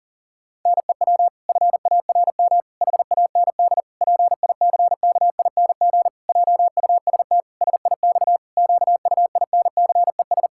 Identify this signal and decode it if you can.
Morse code, heard as a keyed tone, about 40 wpm, 700 Hz. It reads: NEW FARM HAND PICKING JUST SIX QUINCES
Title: Pangram Challenge v2 - 40wpm